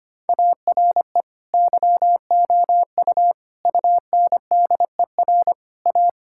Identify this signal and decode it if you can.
Morse code, heard as a keyed tone, about 25 wpm, 700 Hz. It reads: ARE YOU UNDER A
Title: Sentences from Top 1000 Words - Review Part 6 - 25wpm